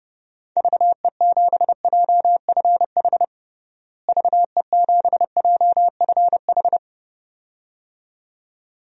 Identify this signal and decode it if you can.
Morse code, heard as a keyed tone, about 31 wpm, 700 Hz. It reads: VE7JFH VE7JFH
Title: DXCC-100 Call Signs - 6 Characters - Repeated 2x - 30wpm